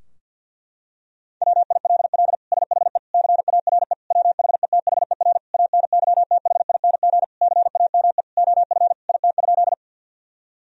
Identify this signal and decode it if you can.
Morse code, heard as a keyed tone, about 53 wpm, 700 Hz. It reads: WILL SHE CARE WHETHER ANYTHING CAME OF IT?
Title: Sentences from Top 1000 Words - Review Part 3 - 50wpm